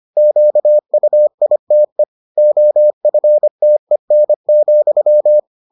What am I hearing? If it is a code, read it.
Morse code, heard as a keyed tone, about 25 wpm, 600 Hz. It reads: QUITE OFTEN,